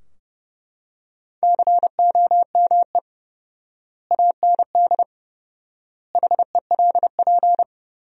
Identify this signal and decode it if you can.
Morse code, heard as a keyed tone, about 30 wpm, 700 Hz. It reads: COME AND HELP
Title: 4x Word-Spacing - Sentences from Top 200 Words - 30wpm